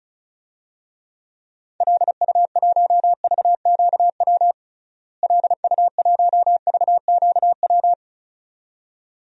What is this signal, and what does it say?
Morse code, heard as a keyed tone, about 36 wpm, 700 Hz. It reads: LU1VQW LU1VQW
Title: DXCC-100 Call Signs - 6 Characters - Repeated 2x - 35wpm